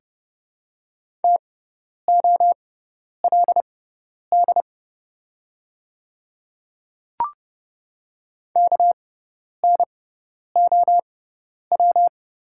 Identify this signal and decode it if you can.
Morse code, heard as a keyed tone, about 30 wpm, spaced as for 10 wpm, 700 Hz. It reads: TOLD KNOW
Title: 6x Character Spacing - Top 500 Words - 30wpm